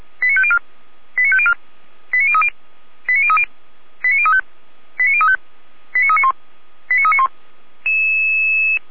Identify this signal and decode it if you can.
Five-tone selective call (ZVEI), heard as quick runs of five tones, tones of about 70 ms. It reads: 89504 89504 89030 89030 89035 89035 89391 89391